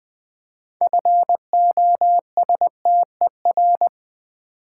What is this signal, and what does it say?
Morse code, heard as a keyed tone, about 21 wpm, 700 Hz. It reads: FOSTER
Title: 6 Letter Words - First 3 Letters Given - 20wpm